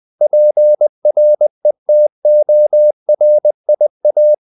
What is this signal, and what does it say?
Morse code, heard as a keyed tone, about 20 wpm, 600 Hz. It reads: PRETORIA
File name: CW / Practice / NationalCapitals20